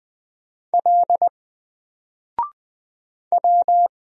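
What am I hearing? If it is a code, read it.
Morse code, heard as a keyed tone, about 20 wpm, 700 Hz. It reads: L W
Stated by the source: Single letter and number Morse drill at 20 wpm, from N4LCD.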